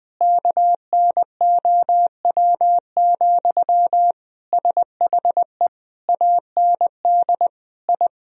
Morse code, heard as a keyed tone, about 20 wpm, 700 Hz. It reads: KNOW, SHE AND I
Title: Sentences from Top 1000 Words - Review Part 3 - 20wpm